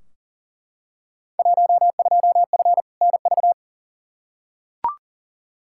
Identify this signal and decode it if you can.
Morse code, heard as a keyed tone, about 40 wpm, 700 Hz. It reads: 12F NV
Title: Field Day Exchanges Encore - 40wpm